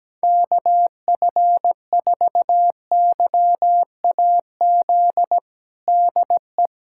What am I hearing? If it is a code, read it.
Morse code, heard as a keyed tone, about 17 wpm, 700 Hz. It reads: KF4YAZ DE
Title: USA Callsign DE Callsign 17wpm